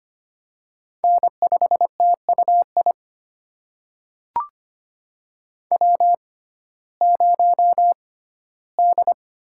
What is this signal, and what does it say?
Morse code, heard as a keyed tone, about 25 wpm, spaced as for 15 wpm, 700 Hz. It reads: N5TUI W0D
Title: USA Call Signs - 5 Characters - 6x Character Spacing - 25wpm